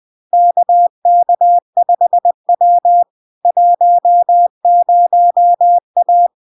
Morse code, heard as a keyed tone, about 20 wpm, 700 Hz. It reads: KK5W 10A
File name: LARK-FD-2022-Log-in-CW-20wpm-700-Hz